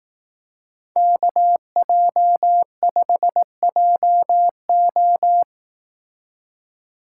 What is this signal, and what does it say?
Morse code, heard as a keyed tone, about 18 wpm, 700 Hz. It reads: KJ5JO
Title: Speed Racing - USA 2x2 Call Signs 15wpm